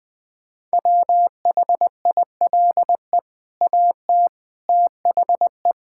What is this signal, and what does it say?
Morse code, heard as a keyed tone, about 21 wpm, 700 Hz. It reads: WHILE AT THE
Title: Build A Phrase - 5-Word Phrases - 20wpm